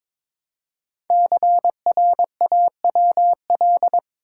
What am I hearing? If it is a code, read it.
Morse code, heard as a keyed tone, about 22 wpm, 700 Hz. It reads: CRAWL
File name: 5 Letter Words - First 3 Letters Given - 22wpm